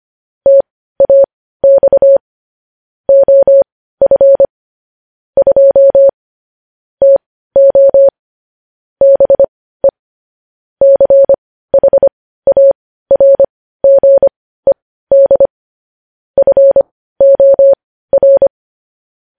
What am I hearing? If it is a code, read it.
Morse code, heard as a keyed tone, about 25 wpm, spaced as for 15 wpm, 550 Hz. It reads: TAX OF 2 TO BE CHARGED FOR
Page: news-headlines-in-morse-code-at-15-wpm